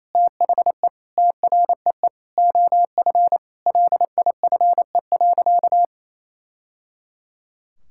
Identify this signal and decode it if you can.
Morse code, heard as a keyed tone, about 28 wpm, 700 Hz. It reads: THE TREE OF LIFE.
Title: Sentences from Top 1000 Words - Review Part 6 - 28wpm